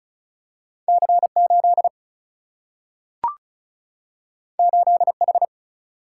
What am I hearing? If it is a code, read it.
Morse code, heard as a keyed tone, about 34 wpm, 700 Hz. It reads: C8 8H
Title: Single-Letter Single-Number 35wpm